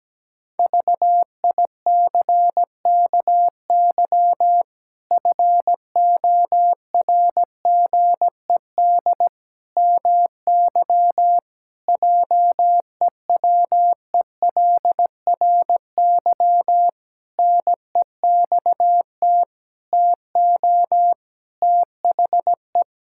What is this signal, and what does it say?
Morse code, heard as a keyed tone, about 17 wpm, 700 Hz. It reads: VICKY FORGED MY JEWELRY NEXT TO THE
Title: Pangram Challenge v2 - 17wpm